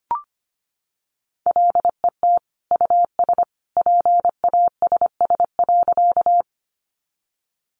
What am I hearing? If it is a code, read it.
Morse code, heard as a keyed tone, about 25 wpm, 700 Hz. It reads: LET US PASS.
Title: Sentences from Top 400 Words 25wpm